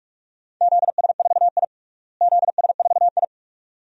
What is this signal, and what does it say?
Morse code, heard as a keyed tone, about 46 wpm, 700 Hz. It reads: ZS4I ZS4I
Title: DXCC-100 Call Signs - 4 Characters - Repeated 2x - 45wpm